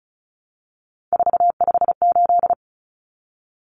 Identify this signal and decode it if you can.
Morse code, heard as a keyed tone, about 34 wpm, 700 Hz. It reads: 458
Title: Numbers - 3-Digit 35wpm